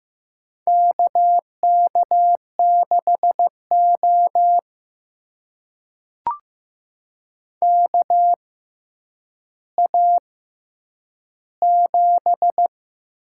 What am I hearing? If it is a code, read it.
Morse code, heard as a keyed tone, about 15 wpm, spaced as for 9 wpm, 700 Hz. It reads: KK6O KA7
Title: USA Call Signs - 6x Character Spacing - 15wpm